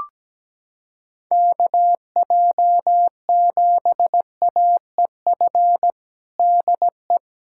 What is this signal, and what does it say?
Morse code, heard as a keyed tone, about 17 wpm, 700 Hz. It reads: KJ7AEF DE
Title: USA Callsign DE Callsign 17wpm